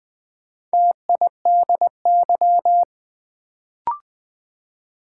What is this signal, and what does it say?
Morse code, heard as a keyed tone, about 20 wpm, 700 Hz. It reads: TIDY
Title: Build A Word - 2 Letters At A Time - 4-Letter Words - 20wpm